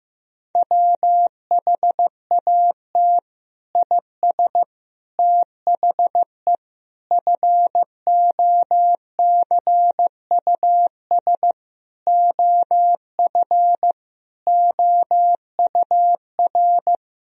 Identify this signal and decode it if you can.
Morse code, heard as a keyed tone, about 15 wpm, 700 Hz. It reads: WHAT IS THE FOCUS OF OUR
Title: Sentences from Top 1000 Words - Review Part 4 - 15wpm